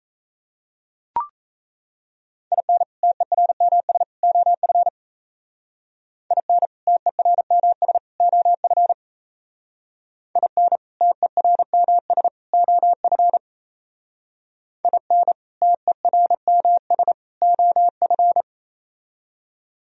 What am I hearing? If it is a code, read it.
Morse code, heard as a keyed tone, about 37 wpm, 700 Hz. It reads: IN TERMS OF IN TERMS OF IN TERMS OF IN TERMS OF